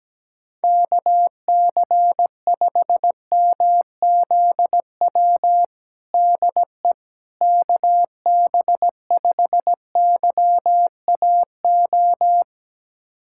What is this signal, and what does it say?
Morse code, heard as a keyed tone, about 17 wpm, 700 Hz. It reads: KC5MZW DE KB5YAO
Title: USA Callsign DE Callsign - 6 Characters - 17wpm